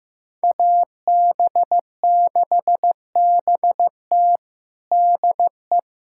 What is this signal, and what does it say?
Morse code, heard as a keyed tone, about 15 wpm, 700 Hz. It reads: AB6BT DE
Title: USA Callsign DE Callsign 15wpm